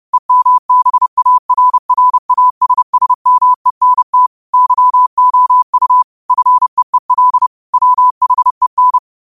Morse code, heard as a keyed tone, about 30 wpm, 1000 Hz. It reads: EMBARRASSMENT YOU FEEL WHEN
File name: qotdfiles / 30wpm